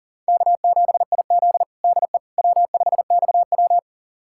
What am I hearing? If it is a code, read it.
Morse code, heard as a keyed tone, about 37 wpm, 700 Hz. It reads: K7IZ DE W5XW
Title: USA Callsign DE Callsign - 4 Characters - 40wpm